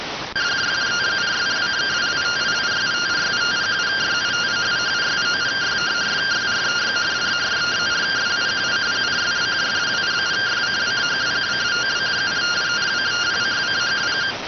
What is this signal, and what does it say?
Two-tone teleprinter signal, about 170 Hz shift.